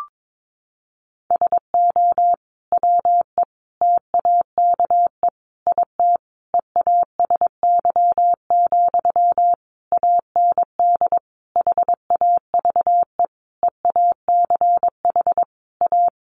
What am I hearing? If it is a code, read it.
Morse code, heard as a keyed tone, about 22 wpm, 700 Hz. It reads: SO WE TAKE IT EASY, AND HAVE EACH A
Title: Sentences from Top 700 Words 22wpm